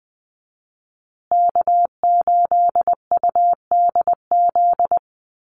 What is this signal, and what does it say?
Morse code, heard as a keyed tone, about 20 wpm, 700 Hz. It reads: K8UDZ
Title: USA Call Signs Repeated 3x 20wpm